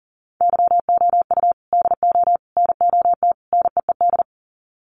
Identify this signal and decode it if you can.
Morse code, heard as a keyed tone, about 38 wpm, 700 Hz. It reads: YOU DO NOT NEED
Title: Sets of 4 Words 40wpm